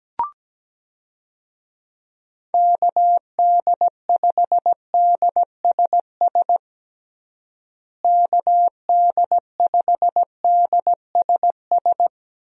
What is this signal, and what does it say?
Morse code, heard as a keyed tone, about 17 wpm, 700 Hz. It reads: KD5DSS KD5DSS
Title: USA Call Signs - 6 Characters - Repeated 2x - 17wpm